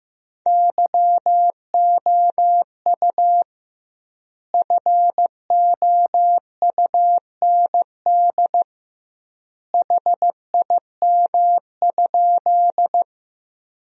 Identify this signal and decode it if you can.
Morse code, heard as a keyed tone, about 15 wpm, 700 Hz. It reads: YOU FOUND HIM?
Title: 2x Word-Spacing - Sentences from Top 300 Words - 15wpm